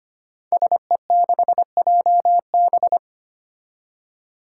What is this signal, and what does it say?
Morse code, heard as a keyed tone, about 25 wpm, 700 Hz. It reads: SE6JB
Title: Five Challenging Letter-Number Combinations 25wpm